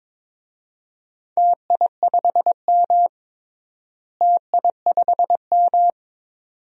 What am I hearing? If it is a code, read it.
Morse code, heard as a keyed tone, about 22 wpm, 700 Hz. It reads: TI5M TI5M
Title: DXCC-100 Call Signs - 4 Characters - Repeated 2x - 22wpm